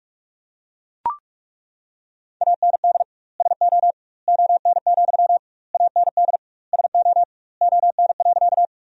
Morse code, heard as a keyed tone, about 41 wpm, 700 Hz. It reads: AND SO ON, AND SO ON.
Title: 2x Word-Spacing - Sentences from Top 100 Words - 45wpm